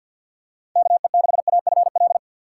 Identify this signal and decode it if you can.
Morse code, heard as a keyed tone, about 52 wpm, 700 Hz. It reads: KE6AFL
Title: USA Call Signs - 6 Characters - 4x Character Spacing - 50wpm